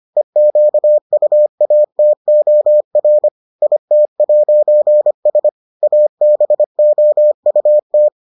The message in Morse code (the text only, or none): EQUATOR IT'S ABOUT